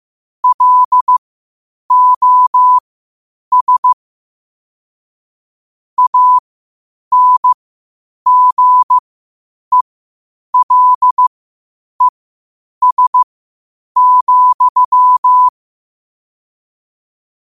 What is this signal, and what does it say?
Morse code, heard as a keyed tone, about 15 wpm, spaced as for 8 wpm, 1000 Hz. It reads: LOS ANGELES,
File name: qotdfiles / 8wpm